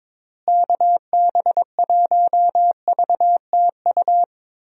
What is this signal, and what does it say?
Morse code, heard as a keyed tone, about 22 wpm, 700 Hz. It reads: KB1VTU
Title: USA Call Signs - 6 Characters - Repeated 3x - 22wpm